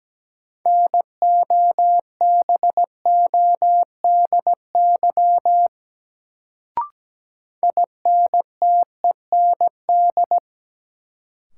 Morse code, heard as a keyed tone, about 17 wpm, 700 Hz. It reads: NOBODY INTEND
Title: Sentences from Top 1600 Words - New Word Drill - 17wpm